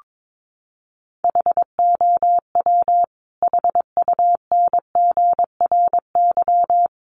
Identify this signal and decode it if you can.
Morse code, heard as a keyed tone, about 22 wpm, 700 Hz. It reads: HOW HUNGRY
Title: Sentences from Top 1400 Words 22wpm